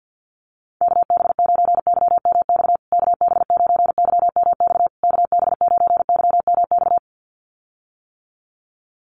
Morse code, heard as a keyed tone, about 50 wpm, 700 Hz. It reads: KB9YMX KB9YMX KB9YMX
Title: USA Call Signs Repeated 3x 50wpm